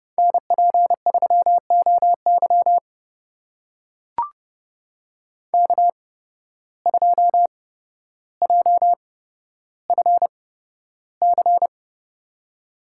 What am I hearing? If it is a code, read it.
Morse code, heard as a keyed tone, about 30 wpm, spaced as for 13 wpm, 700 Hz. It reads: NP3OY K2JFC
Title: USA Call Signs - 5 Characters - 8x Character Spacing - 30wpm